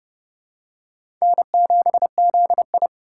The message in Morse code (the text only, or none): N7ZI